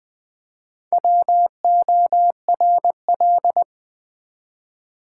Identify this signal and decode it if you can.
Morse code, heard as a keyed tone, about 20 wpm, 700 Hz. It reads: WORL
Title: Build A Word - 1 Letter At A Time - 3 to 5-Letter Words - 20wpm